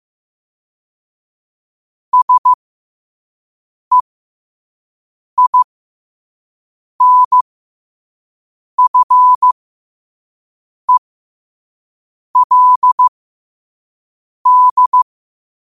Morse code, heard as a keyed tone, about 15 wpm, spaced as for 5 wpm, 1000 Hz. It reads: SEINFELD
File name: qotdfiles / 5wpm